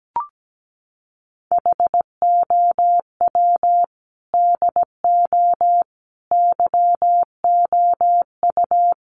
Morse code, heard as a keyed tone, about 17 wpm, 700 Hz. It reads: HOW DO YOU
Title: Sets of 5 Words Encore 17wpm